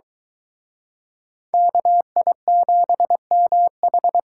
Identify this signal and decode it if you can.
Morse code, heard as a keyed tone, about 22 wpm, 700 Hz. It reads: KI7MH